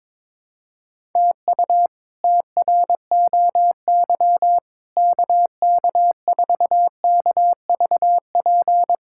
Morse code, heard as a keyed tone, about 22 wpm, 700 Hz. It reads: TU TROY KK4KVP
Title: K1USN SST QSO Practice - 22wpm